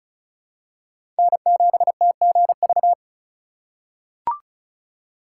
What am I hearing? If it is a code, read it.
Morse code, heard as a keyed tone, about 36 wpm, 700 Hz. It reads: N7TGV
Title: USA Call Signs - 3x Character Spacing - 35wpm